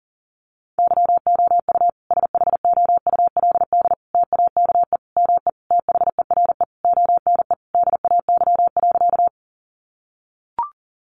Morse code, heard as a keyed tone, about 40 wpm, 700 Hz. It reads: YOU SHOULD TAKE ME THERE ONE DAY.
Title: Sentences from Top 300 Words 40wpm